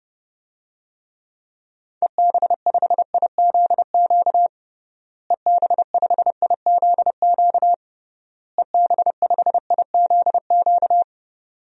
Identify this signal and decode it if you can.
Morse code, heard as a keyed tone, about 30 wpm, 700 Hz. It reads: EB5IZQ EB5IZQ EB5IZQ
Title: DXCC-100 Call Signs - 6 Characters - Repeated 3x - 30wpm